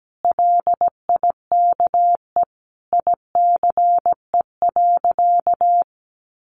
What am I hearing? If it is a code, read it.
Morse code, heard as a keyed tone, about 17 wpm, 700 Hz. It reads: LIKE ICE.
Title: Sentences from Top 1000 Words 17wpm